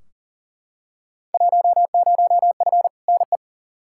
Morse code, heard as a keyed tone, about 40 wpm, 700 Hz. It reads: 10F NE
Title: Field Day Exchanges Encore - 40wpm